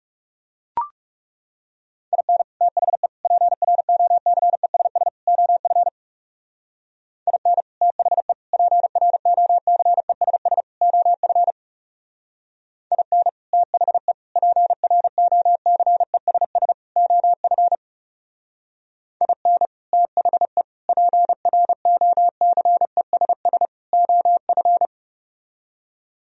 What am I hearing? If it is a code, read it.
Morse code, heard as a keyed tone, about 38 wpm, 700 Hz. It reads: IN THE PROCESS OF IN THE PROCESS OF IN THE PROCESS OF IN THE PROCESS OF